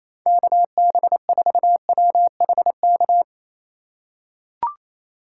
Morse code, heard as a keyed tone, about 28 wpm, 700 Hz. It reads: KB4WHK
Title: USA Call Signs - 6 Characters - 3x Character Spacing - 28wpm